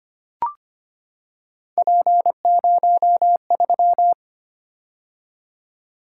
Morse code, heard as a keyed tone, about 25 wpm, 700 Hz. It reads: P03
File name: Three Letter-Number Combinations - 25wpm